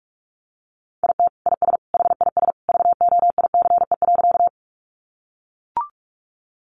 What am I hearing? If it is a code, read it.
Morse code, heard as a keyed tone, about 44 wpm, 700 Hz. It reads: IT IS HIS VOICE.
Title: Sentences from Top 500 Words 45wpm